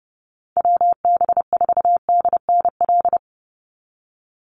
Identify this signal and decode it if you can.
Morse code, heard as a keyed tone, about 29 wpm, 700 Hz. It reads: WB4DNL
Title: Call Signs Repeated 2x - 30wpm